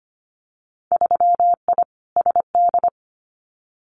Morse code, heard as a keyed tone, about 25 wpm, 690 Hz. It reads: 3I SD
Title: Winter Field Day Exchanges Encore 25wpm